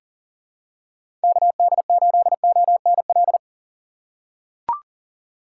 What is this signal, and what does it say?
Morse code, heard as a keyed tone, about 40 wpm, 700 Hz. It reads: KD8ONL
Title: USA Call Signs - 6 Characters - 3x Character Spacing - 40wpm